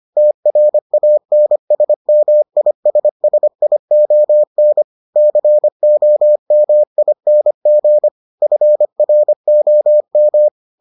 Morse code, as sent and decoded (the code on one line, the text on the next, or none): - .-. .- -. ... -- .. ... ... .. --- -. / -.-. --- -- .. -. --. / ..-. .-. --- --
TRANSMISSION COMING FROM